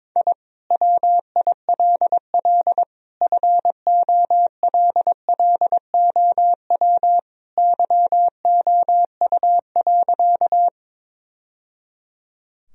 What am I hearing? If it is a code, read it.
Morse code, heard as a keyed tone, about 22 wpm, 700 Hz. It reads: I WILL FOLLOW YOU.
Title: Sentences from Top 1000 Words - Review Part 1 - 22wpm